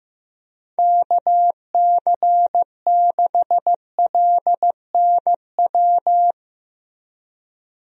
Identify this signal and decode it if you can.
Morse code, heard as a keyed tone, about 15 wpm, 700 Hz. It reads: KC6LNW
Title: USA Call Signs - 6 Characters - 15wpm